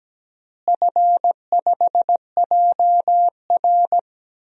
Morse code, heard as a keyed tone, about 17 wpm, 700 Hz. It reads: F5JR
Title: DXCC-100 Call Signs - 4 Characters - Repeated 3x - 17wpm